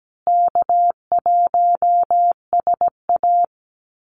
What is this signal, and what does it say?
Morse code, heard as a keyed tone, about 17 wpm, 700 Hz. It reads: K1SA
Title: Easy Call Signs Repeated 3x - 17wpm